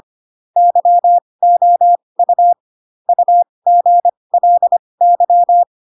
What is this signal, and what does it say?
Morse code, heard as a keyed tone, about 25 wpm, spaced as for 20 wpm, 700 Hz. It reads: YOU UGLY
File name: Pinocchio / Pinocchio ch4 15 25 ik1hge